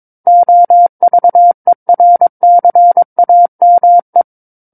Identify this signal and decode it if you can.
Morse code, heard as a keyed tone, about 22 wpm, 700 Hz. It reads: OVERCAME
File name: Book One / 13-Chapter 13 - 22 WPM